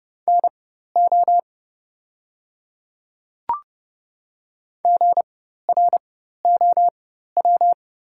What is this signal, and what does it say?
Morse code, heard as a keyed tone, about 30 wpm, spaced as for 14 wpm, 700 Hz. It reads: NO GROW
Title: 4x Character Spacing - Top 500 Words - 30wpm